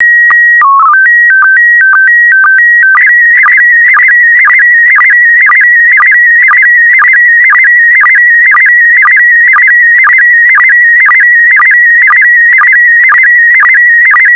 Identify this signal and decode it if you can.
SSTV (slow-scan television), PD 120.
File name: SSTV PD120 Leo